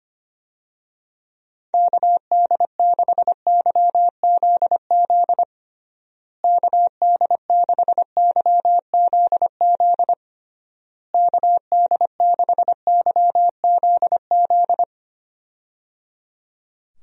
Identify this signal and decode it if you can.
Morse code, heard as a keyed tone, about 25 wpm, 700 Hz. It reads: KD6YZZ KD6YZZ KD6YZZ
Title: USA Call Signs - 6 Characters - Repeated 3x - 25wpm